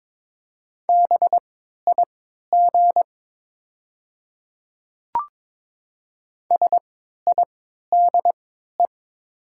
Morse code, heard as a keyed tone, about 22 wpm, spaced as for 12 wpm, 700 Hz. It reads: BIG SIDE
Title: 3x Character Spacing - Top 300 Words - 22wpm